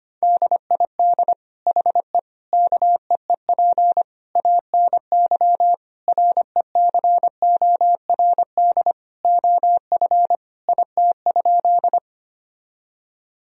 Morse code, heard as a keyed tone, about 25 wpm, 700 Hz. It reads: DID HE KEEP ANY RECORD OF IT?
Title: Sentences from Top 1000 Words - Review Part 3 - 25wpm